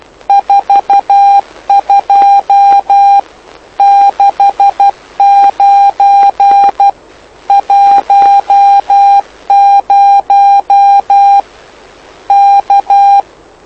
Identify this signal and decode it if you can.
Morse code, heard as a keyed tone, about 12 wpm, 800 Hz. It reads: 42 69 10 K
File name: lu4aao / lu7aa 09dic2011 exp planeador vuelo2